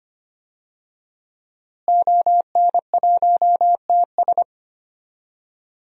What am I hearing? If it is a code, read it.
Morse code, heard as a keyed tone, about 25 wpm, 700 Hz. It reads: ON1TS
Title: DXCC-100 Call Signs - 5 Characters - Repeated 2x - 25wpm